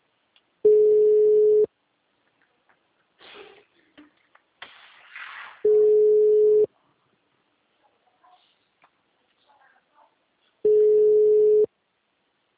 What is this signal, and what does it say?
Ringback tone (Europe, 425 Hz).